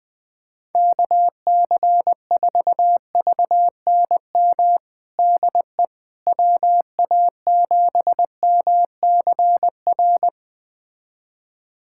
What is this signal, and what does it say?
Morse code, heard as a keyed tone, about 20 wpm, 700 Hz. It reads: KC4VNM DE WA7MCR
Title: USA Callsign DE Callsign - 6 Characters - 20wpm